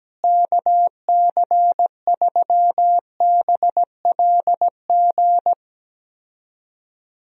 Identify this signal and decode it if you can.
Morse code, heard as a keyed tone, about 17 wpm, 700 Hz. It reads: KC3BLG
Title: USA Call Signs - 6 Characters - Repeated 3x - 17wpm